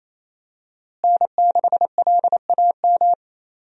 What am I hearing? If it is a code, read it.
Morse code, heard as a keyed tone, about 28 wpm, 700 Hz. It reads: N6LAM
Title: USA Call Signs - 5 Characters - Repeated 3x - 28wpm